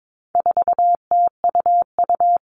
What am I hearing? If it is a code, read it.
Morse code, heard as a keyed tone, about 22 wpm, 700 Hz. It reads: 4TUU
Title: All Cut Numbers - 4-Digit 22wpm